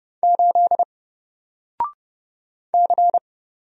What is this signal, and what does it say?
Morse code, heard as a keyed tone, about 31 wpm, 700 Hz. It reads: 8 C
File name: Single-Letter-Number-30wpm